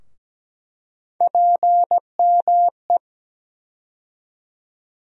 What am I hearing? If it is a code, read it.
Morse code, heard as a keyed tone, about 17 wpm, 700 Hz. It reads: PME
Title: USA Call Sign Postfixes - 3 Letters 17wpm